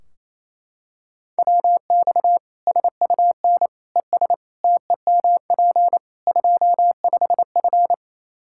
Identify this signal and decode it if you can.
Morse code, heard as a keyed tone, about 28 wpm, 700 Hz. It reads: WX SUN ES TEMP 25F
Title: Practice QSO - Exchange 2 - Info - 28wpm